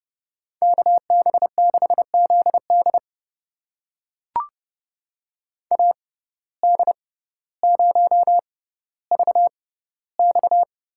Morse code, heard as a keyed tone, about 32 wpm, spaced as for 16 wpm, 700 Hz. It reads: KB6ZD AD0VX
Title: USA Call Signs - 5 Characters - 6x Character Spacing - 30wpm